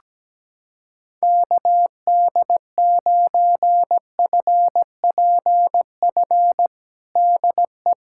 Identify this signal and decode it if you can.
Morse code, heard as a keyed tone, about 17 wpm, 700 Hz. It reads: KD9FPF DE
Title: USA Callsign DE Callsign - 6 Characters - 17wpm